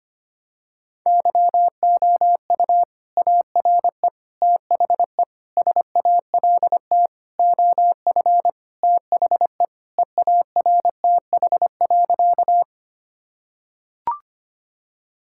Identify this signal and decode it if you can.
Morse code, heard as a keyed tone, about 25 wpm, 700 Hz. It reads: YOU ARE THE SALT OF THE EARTH.
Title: Sentences from Top 1000 Words - Review Part 5 - 25wpm